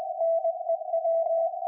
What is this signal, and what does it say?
Morse code, heard as a keyed tone, about 20 wpm, 660 Hz, with radio noise. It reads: NEW
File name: IC-7100